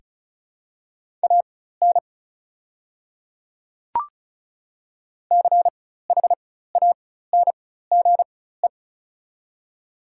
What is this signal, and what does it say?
Morse code, heard as a keyed tone, about 34 wpm, spaced as for 16 wpm, 700 Hz. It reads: AN CHANGE
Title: 4x Character Spacing - Top 200 Words - 35wpm